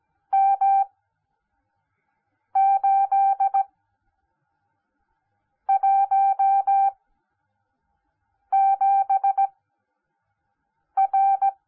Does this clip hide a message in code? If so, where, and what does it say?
Yes: Morse code from 0.3 to 11.5 s, heard as a keyed tone, about 17 wpm, spaced as for 5 wpm, 780 Hz. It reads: M817R